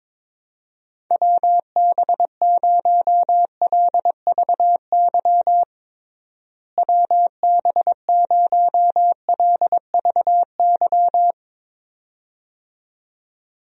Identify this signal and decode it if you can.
Morse code, heard as a keyed tone, about 21 wpm, 700 Hz. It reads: WB0LVY WB0LVY
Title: USA Call Signs - 6 Characters - Repeated 2x - 22wpm